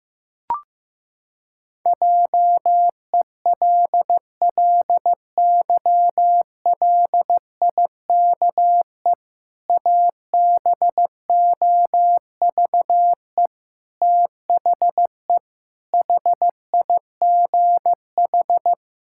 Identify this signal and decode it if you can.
Morse code, heard as a keyed tone, about 15 wpm, 700 Hz. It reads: JELLYLIKE ABOVE THE HIGH
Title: Pangram Challenge v2 - 15wpm